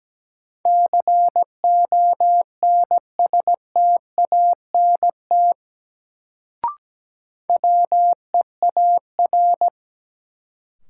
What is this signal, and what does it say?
Morse code, heard as a keyed tone, about 17 wpm, 700 Hz. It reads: CONSTANT WEAR
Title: Sentences from Top 1500 Words - New Word Drill Alt - 17wpm